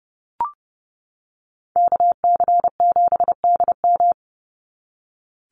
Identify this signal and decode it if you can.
Morse code, heard as a keyed tone, about 30 wpm, 700 Hz. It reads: KC7DM
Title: NA SOTA Activator Callsigns 30wpm